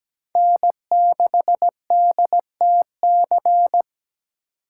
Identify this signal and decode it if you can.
Morse code, heard as a keyed tone, about 17 wpm, 700 Hz. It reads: N6DTC
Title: USA Call Signs - 5 Characters - 8x Character Spacing - 17wpm